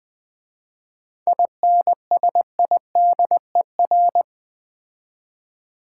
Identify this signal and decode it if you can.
Morse code, heard as a keyed tone, about 20 wpm, 700 Hz. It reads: INSIDER
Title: 7 Letter Words - First 3 Letters Given - 20wpm